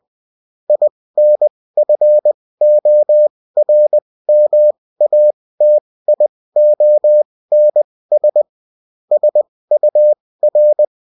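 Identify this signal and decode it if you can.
Morse code, heard as a keyed tone, about 20 wpm, spaced as for 16 wpm, 600 Hz. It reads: INFORMATIONS SUR
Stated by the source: Morse code bulletin of the UFT.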